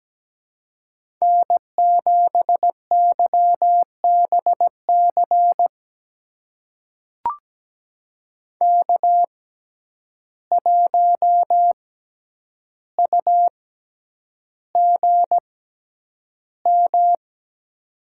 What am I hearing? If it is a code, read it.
Morse code, heard as a keyed tone, about 17 wpm, spaced as for 9 wpm, 700 Hz. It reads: N7YBC K1UGM
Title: USA Call Signs - 6x Character Spacing - 17wpm